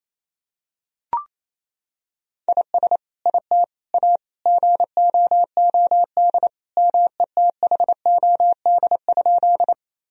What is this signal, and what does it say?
Morse code, heard as a keyed tone, about 28 wpm, 700 Hz. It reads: IS IT A GOOD METHOD?
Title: Sentences from Top 1000 Words - Review Part 3 - 28wpm